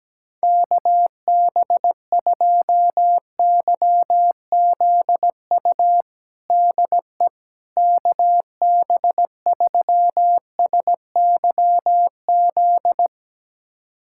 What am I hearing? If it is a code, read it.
Morse code, heard as a keyed tone, about 17 wpm, 700 Hz. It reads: KB2YZU DE KB3SYZ
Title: USA Callsign DE Callsign 17wpm